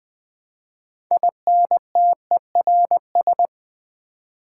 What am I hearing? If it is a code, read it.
Morse code, heard as a keyed tone, about 20 wpm, 700 Hz. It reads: INTERS
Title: Build A Word - 6 Letters At A Time - 12-Letter Words - 20wpm